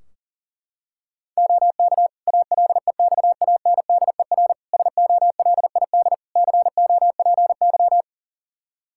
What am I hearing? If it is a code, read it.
Morse code, heard as a keyed tone, about 40 wpm, 700 Hz. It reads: OK ALEXANDER SOLID COPY
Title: K1USN SST QSO Practice 40wpm